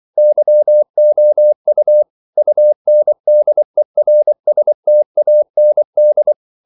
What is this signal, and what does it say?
Morse code, heard as a keyed tone, about 24 wpm, 600 Hz. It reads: YOU UNDERSTAND
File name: CW / Practice / EnglishQuotations0424